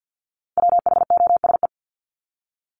Morse code, heard as a keyed tone, about 48 wpm, 700 Hz. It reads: WHOSE
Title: Top 1001-1250 Words 50wpm